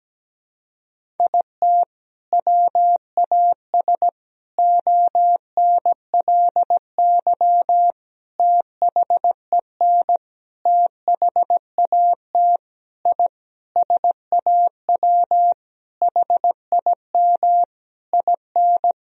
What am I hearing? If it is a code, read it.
Morse code, heard as a keyed tone, about 17 wpm, 700 Hz. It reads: IT WAS ONLY THEN THAT I SAW HIM IN
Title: Sentences from Top 1000 Words - Review Part 3 - 17wpm